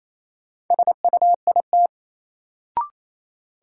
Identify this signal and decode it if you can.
Morse code, heard as a keyed tone, about 28 wpm, 700 Hz. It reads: SUIT
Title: Sentences from Top 1600 Words - New Word Drill - 28wpm